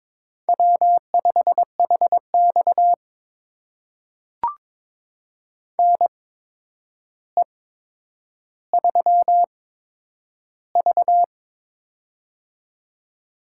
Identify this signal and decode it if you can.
Morse code, heard as a keyed tone, about 21 wpm, spaced as for 9 wpm, 700 Hz. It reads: W5HX NE3V
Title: USA Call Signs - 4 Characters - 8x Character Spacing - 22wpm